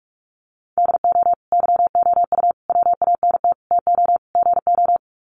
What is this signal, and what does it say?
Morse code, heard as a keyed tone, about 44 wpm, 690 Hz. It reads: DO YOU WANT TO GO
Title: Sets of 5 Words Encore2 45wpm